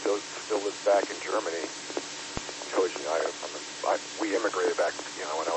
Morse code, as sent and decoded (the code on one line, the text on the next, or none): none